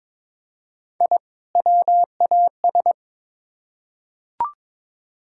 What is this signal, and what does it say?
Morse code, heard as a keyed tone, about 23 wpm, 700 Hz. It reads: I WAS
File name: Build A Phrase - 2-Word Phrases - 22wpm